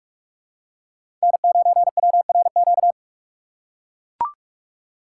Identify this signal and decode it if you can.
Morse code, heard as a keyed tone, about 41 wpm, 700 Hz. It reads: N9WRQ
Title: USA Call Signs - 5 Characters - 3x Character Spacing - 45wpm